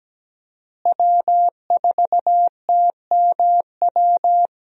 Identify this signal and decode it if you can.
Morse code, heard as a keyed tone, about 17 wpm, 700 Hz. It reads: W4TMW
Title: USA Call Signs - 5 Characters - Repeated 2x - 17wpm